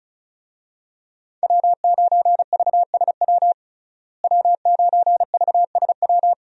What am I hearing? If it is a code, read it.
Morse code, heard as a keyed tone, about 34 wpm, 700 Hz. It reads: W9VSW W9VSW
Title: USA Call Signs - 5 Characters - Repeated 3x - 35wpm